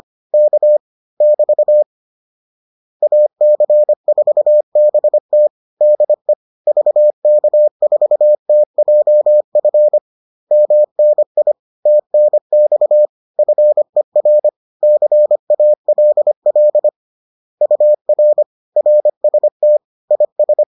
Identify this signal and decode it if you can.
Morse code, heard as a keyed tone, about 25 wpm, 600 Hz. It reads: K <BT> AC4BT DE VK4TJF MNI TNX FER CALL UR RST IS